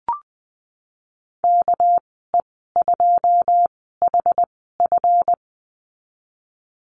Morse code, heard as a keyed tone, about 20 wpm, spaced as for 15 wpm, 700 Hz. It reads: KE2HF
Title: 231 Lesson 37g - Letter x - Call Signs 20wpm